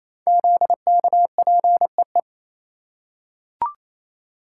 Morse code, heard as a keyed tone, about 27 wpm, 700 Hz. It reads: ZKPEE
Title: Code Groups - 5 Letters - 28wpm